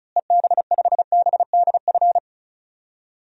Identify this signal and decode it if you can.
Morse code, heard as a keyed tone, about 34 wpm, 700 Hz. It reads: EB5BDF
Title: DXCC-100 Call Signs - 6 Characters - Repeated 3x - 35wpm